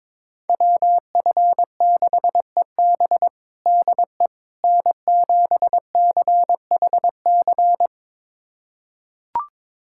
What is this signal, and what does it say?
Morse code, heard as a keyed tone, about 21 wpm, 700 Hz. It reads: WF6EB DE N7CHC
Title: USA Callsign DE Callsign - 5 Characters - 22wpm